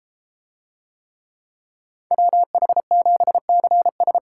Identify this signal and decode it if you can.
Morse code, heard as a keyed tone, about 32 wpm, 700 Hz. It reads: WH7CS